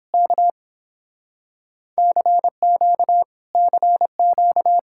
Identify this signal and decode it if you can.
Morse code, heard as a keyed tone, about 29 wpm, 700 Hz. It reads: K CQ CQ